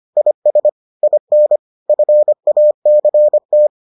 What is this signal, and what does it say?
Morse code, heard as a keyed tone, about 25 wpm, 600 Hz. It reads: IS IN FACT